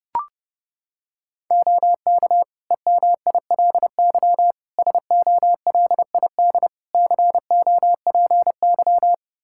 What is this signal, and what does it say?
Morse code, heard as a keyed tone, about 32 wpm, 700 Hz. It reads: OK EMILY SOLID COPY
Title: Practice QSO - Exchange 3 - More Info - 30wpm